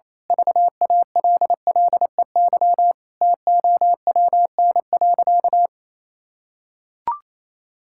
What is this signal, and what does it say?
Morse code, heard as a keyed tone, about 28 wpm, 700 Hz. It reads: VALLEY TOWN.
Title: Pangram Challenge v2 - 28wpm